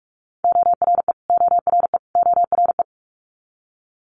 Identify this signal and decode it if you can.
Morse code, heard as a keyed tone, about 46 wpm, 700 Hz. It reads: ORE ORE ORE
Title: Multi-letter Phonemes Repeated 3x 45wpm